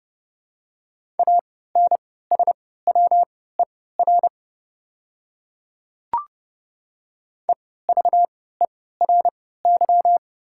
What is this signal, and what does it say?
Morse code, heard as a keyed tone, about 30 wpm, spaced as for 17 wpm, 700 Hz. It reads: ANSWER EVERY
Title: 3x Character Spacing - Top 300 Words - 30wpm